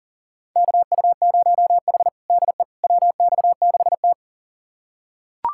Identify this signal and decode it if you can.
Morse code, heard as a keyed tone, about 40 wpm, 700 Hz. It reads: KU0H DE WX6T
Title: USA Callsign DE Callsign - 4 Characters - 40wpm